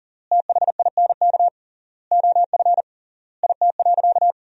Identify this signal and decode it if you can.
Morse code, heard as a keyed tone, about 40 wpm, 700 Hz. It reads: THINK OF IT.
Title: 3x Word-Spacing - Sentences from Top 200 Words - 40wpm